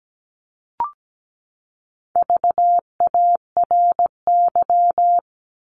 Morse code, heard as a keyed tone, about 17 wpm, 700 Hz. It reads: VARY
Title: Top 1001-1250 Words 17wpm